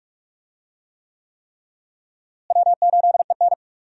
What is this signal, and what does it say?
Morse code, heard as a keyed tone, about 43 wpm, 700 Hz. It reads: W8EN
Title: USA Call Signs - 4 Characters - Repeated 3x - 45wpm